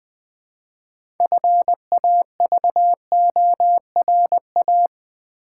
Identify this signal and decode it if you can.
Morse code, heard as a keyed tone, about 20 wpm, 700 Hz. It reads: FAVORA
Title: Build A Word - 3 Letters At A Time - 9-Letter Words - 20wpm